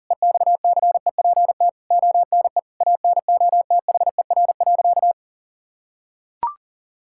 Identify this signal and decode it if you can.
Morse code, heard as a keyed tone, about 37 wpm, 700 Hz. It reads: EXCEPT ONE ANOTHER.
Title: Sentences from Top 1600 Words 40wpm